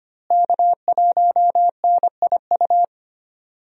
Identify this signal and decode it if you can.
Morse code, heard as a keyed tone, about 25 wpm, 700 Hz. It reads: K1NIU
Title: USA Call Signs - 5 Characters - Repeated 2x - 25wpm